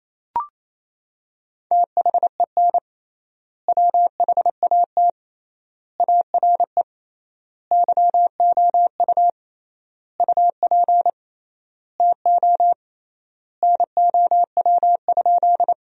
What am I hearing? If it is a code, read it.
Morse code, heard as a keyed tone, about 28 wpm, 700 Hz. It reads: THEN WHAT ARE YOU UP TO NOW?
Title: 3x Word-Spacing - Sentences from Top 100 Words - 28wpm